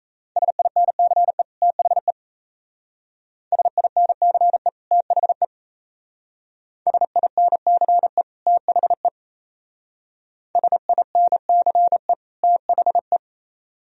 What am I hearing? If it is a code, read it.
Morse code, heard as a keyed tone, about 33 wpm, 700 Hz. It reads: SINCE THE SINCE THE SINCE THE SINCE THE
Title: Speed-Racing - Sets of 2 Words 28wpm